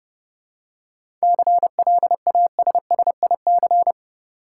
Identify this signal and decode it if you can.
Morse code, heard as a keyed tone, about 30 wpm, 700 Hz. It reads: CLASSIC
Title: Seven Letter Words - 30wpm